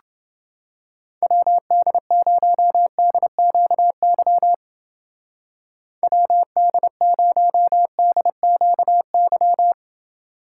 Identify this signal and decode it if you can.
Morse code, heard as a keyed tone, about 29 wpm, 700 Hz. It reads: WD0DQY WD0DQY